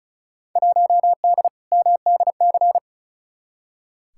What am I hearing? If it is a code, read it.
Morse code, heard as a keyed tone, about 36 wpm, 700 Hz. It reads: 1D MDC
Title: Field Day Exchanges - 35wpm